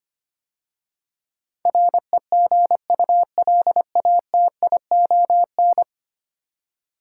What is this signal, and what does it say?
Morse code, heard as a keyed tone, about 25 wpm, 690 Hz. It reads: REGULATION
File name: 10 Letter Words - First 3 Letters Given - 25wpm